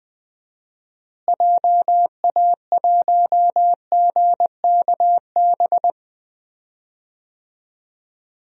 Morse code, heard as a keyed tone, about 20 wpm, 700 Hz. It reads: JA1GKB
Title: DXCC-100 Call Signs - 6 Characters - Repeated 2x - 20wpm